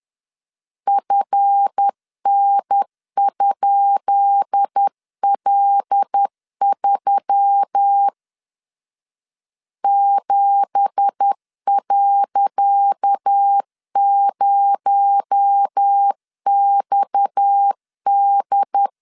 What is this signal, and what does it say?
Morse code, heard as a keyed tone, about 10 wpm, 800 Hz. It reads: FN?L3 7.0XD
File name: LEZIONI / 10 SIGNS / 10C DOT 10wpm 2940